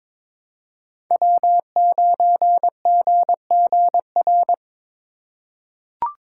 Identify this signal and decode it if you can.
Morse code, heard as a keyed tone, about 22 wpm, 700 Hz. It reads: W9GGR
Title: USA Call Signs - 5 Characters - 2x Character Spacing - 22wpm